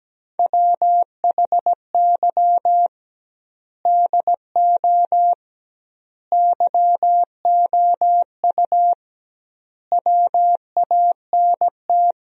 Morse code, heard as a keyed tone, about 17 wpm, 700 Hz. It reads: WHY DO YOU WANT
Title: 2x Word-Spacing - Sentences from Top 200 Words - 17wpm